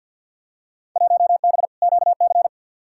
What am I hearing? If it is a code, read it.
Morse code, heard as a keyed tone, about 51 wpm, 700 Hz. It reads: 1B QC
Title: Field Day Exchanges - 50wpm